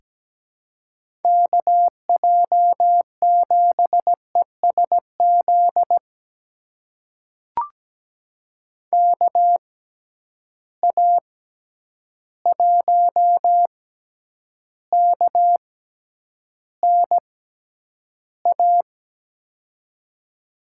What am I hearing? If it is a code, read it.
Morse code, heard as a keyed tone, about 17 wpm, spaced as for 9 wpm, 700 Hz. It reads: KJ7ESZ KA1KNA
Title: USA Call Signs - 6 Characters - 6x Character Spacing - 17wpm